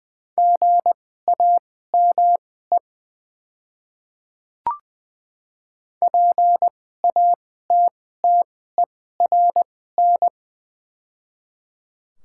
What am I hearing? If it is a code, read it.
Morse code, heard as a keyed tone, about 20 wpm, spaced as for 15 wpm, 690 Hz. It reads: GAME PATTERN
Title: 2x Character Spacing - Top 500 Words - 20wpm